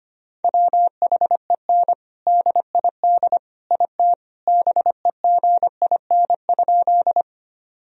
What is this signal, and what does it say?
Morse code, heard as a keyed tone, about 25 wpm, 700 Hz. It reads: WHEN DID IT BEGIN?
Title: Sentences from Top 1000 Words - Review Part 1 - 25wpm